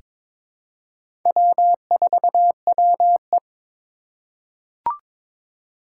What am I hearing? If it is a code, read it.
Morse code, heard as a keyed tone, about 22 wpm, 700 Hz. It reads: W4WE
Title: Four Challenging Letter-Number Combinations 22wpm